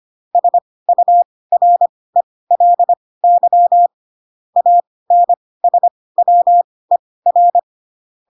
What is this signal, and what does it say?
Morse code, heard as a keyed tone, about 25 wpm, spaced as for 18 wpm, 700 Hz. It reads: SURELY ANSWER
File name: Pinocchio / Pinocchio ch6 12 25 ik1hge